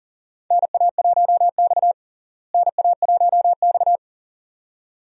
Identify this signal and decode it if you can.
Morse code, heard as a keyed tone, about 40 wpm, 700 Hz. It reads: NA1X NA1X
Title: USA Call Signs - 4 Characters - Repeated 2x - 40wpm